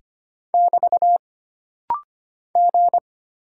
Morse code, heard as a keyed tone, about 25 wpm, 700 Hz. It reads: <BT> G
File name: Single Character (Letter:Number:Punctuation:Prosign) - Rapid-Fire - 25wpm